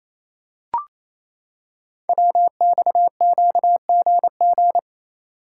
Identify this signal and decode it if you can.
Morse code, heard as a keyed tone, about 28 wpm, 700 Hz. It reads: WXQGG
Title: Code Groups - 5 Letters - 28wpm